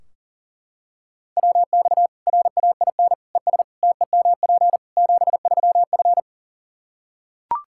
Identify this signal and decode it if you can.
Morse code, heard as a keyed tone, about 40 wpm, 700 Hz. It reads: WX RAIN ES TEMP 73F
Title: Practice QSO - Exchange 2 - Info - 40wpm